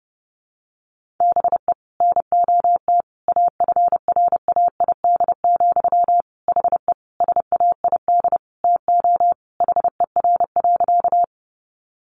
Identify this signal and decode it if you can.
Morse code, heard as a keyed tone, about 30 wpm, 700 Hz. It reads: BE NOT AFRAID, HE SAID TO HER.
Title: Sentences from Top 1500 Words 30wpm